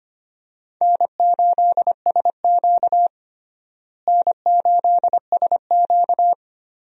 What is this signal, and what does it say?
Morse code, heard as a keyed tone, about 25 wpm, 700 Hz. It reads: N8SQ N8SQ
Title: USA Call Signs - 4 Characters - Repeated 2x - 25wpm